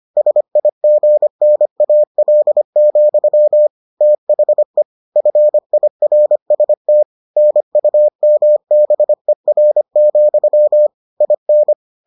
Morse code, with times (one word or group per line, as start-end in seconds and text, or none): 0.2-3.7 SIGNAL,
4.0-4.8 THE
5.2-7.0 FIRST
7.4-10.9 NUMBER,
11.2-11.7 IN